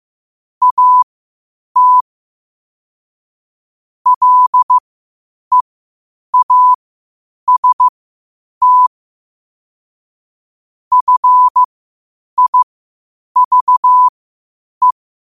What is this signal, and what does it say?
Morse code, heard as a keyed tone, about 15 wpm, spaced as for 8 wpm, 1000 Hz. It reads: AT LEAST FIVE
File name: qotdfiles / 8wpm